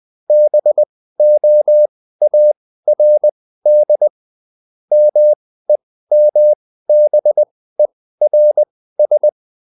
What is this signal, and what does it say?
Morse code, heard as a keyed tone, about 20 wpm, spaced as for 14 wpm, 600 Hz. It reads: BOARD MEMBERS